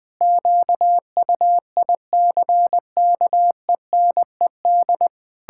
Morse code, heard as a keyed tone, about 21 wpm, 700 Hz. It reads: QUICKENED
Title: Pangram Challenge v2 - 20wpm